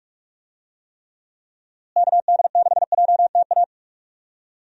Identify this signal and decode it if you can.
Morse code, heard as a keyed tone, about 42 wpm, 700 Hz. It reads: KD6JTA
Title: USA Call Signs - 6 Characters - Repeated 3x - 45wpm